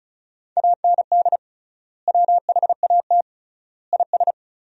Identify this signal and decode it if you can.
Morse code, heard as a keyed tone, about 36 wpm, 700 Hz. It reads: AND WHAT IS
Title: 3x Word-Spacing - Sentences from Top 100 Words - 35wpm